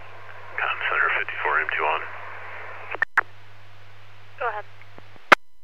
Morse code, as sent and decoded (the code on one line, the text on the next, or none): none